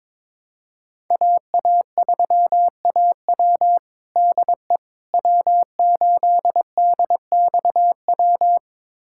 Morse code, heard as a keyed tone, about 22 wpm, 700 Hz. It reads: AA3AW DE W8DXW
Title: USA Callsign DE Callsign - 5 Characters - 22wpm